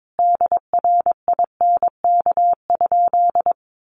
Morse code, heard as a keyed tone, about 22 wpm, 700 Hz. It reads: DRINK?
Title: Sentences from Top 1500 Words 22wpm